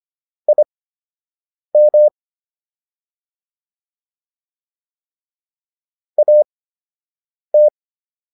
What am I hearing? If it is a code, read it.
Morse code, heard as a keyed tone, about 25 wpm, spaced as for 7 wpm, 600 Hz. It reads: IM AT